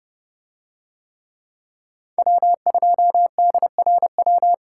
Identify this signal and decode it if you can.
Morse code, heard as a keyed tone, about 30 wpm, 700 Hz. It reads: W2DRW